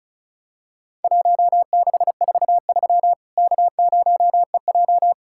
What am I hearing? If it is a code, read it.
Morse code, heard as a keyed tone, about 36 wpm, 700 Hz. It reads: 1643 K0EJ
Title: Worked All Europe DX Contest - QTC Exchange - 35wpm